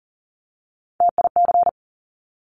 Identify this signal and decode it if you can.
Morse code, heard as a keyed tone, about 42 wpm, 690 Hz. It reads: TIC
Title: Top Three Letter Combinations in Words 40wpm